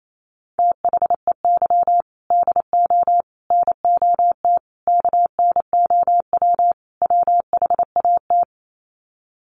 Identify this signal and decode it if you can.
Morse code, heard as a keyed tone, about 28 wpm, 700 Hz. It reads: THEY DO NOT KNOW WHAT
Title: Sets of 5 Words Encore 28wpm